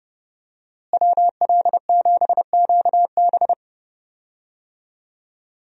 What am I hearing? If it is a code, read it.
Morse code, heard as a keyed tone, about 30 wpm, 700 Hz. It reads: WL7QB
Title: USA Call Signs - 5 Characters - Repeated 2x - 30wpm